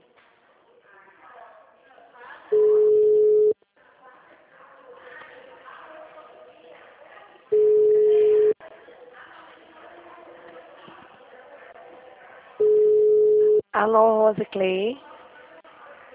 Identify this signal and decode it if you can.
Ringback tone (Europe, 425 Hz), with speech.
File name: out- -2002- - - .43448